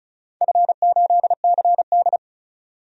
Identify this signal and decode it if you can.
Morse code, heard as a keyed tone, about 36 wpm, 700 Hz. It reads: F8CD
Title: DXCC-100 Call Signs - 4 Characters - 35wpm